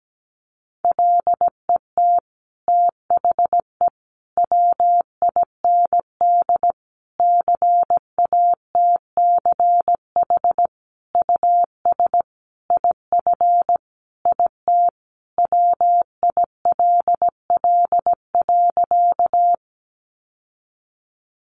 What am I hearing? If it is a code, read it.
Morse code, heard as a keyed tone, about 17 wpm, 700 Hz. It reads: LET THE WIND CATCH US IF IT WILL.
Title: Sentences from Top 1500 Words 17wpm